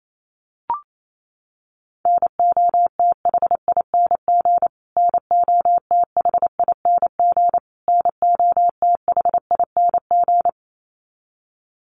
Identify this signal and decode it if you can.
Morse code, heard as a keyed tone, about 28 wpm, 700 Hz. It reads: NOTHING NOTHING NOTHING
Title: Top 500 Words Repeated 28wpm